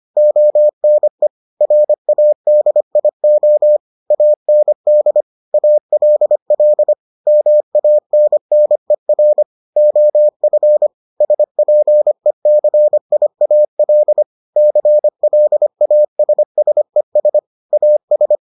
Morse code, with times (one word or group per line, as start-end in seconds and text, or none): 0.2-1.3 ONE
1.6-3.8 RADIO
4.1-5.2 AND
5.5-6.9 ALL
7.3-9.4 MANNER
9.8-10.9 OF
11.2-14.2 SPECIAL
14.6-17.4 CLASSES
17.7-18.4 AS